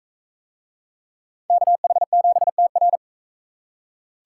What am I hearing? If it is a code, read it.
Morse code, heard as a keyed tone, about 36 wpm, 700 Hz. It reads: KH7TR